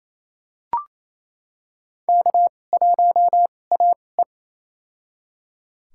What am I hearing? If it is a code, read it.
Morse code, heard as a keyed tone, about 28 wpm, spaced as for 20 wpm, 700 Hz. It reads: K1AE
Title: USA Call Signs - 4 Characters - 2x Character Spacing - 28wpm